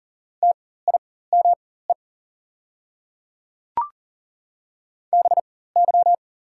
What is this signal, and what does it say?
Morse code, heard as a keyed tone, about 40 wpm, spaced as for 19 wpm, 690 Hz. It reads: TIME BY
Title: 4x Character Spacing - Top 100 Words - 40wpm